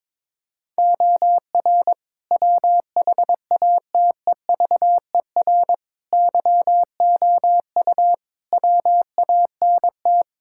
Morse code, heard as a keyed tone, about 22 wpm, 700 Hz. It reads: OR WHATEVER YOU WANT
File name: practice / 22wpm / Sets of 7 Words - 22wpm